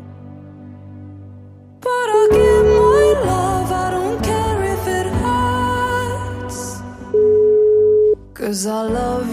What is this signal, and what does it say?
Ringback tone (Europe, 425 Hz), with noise.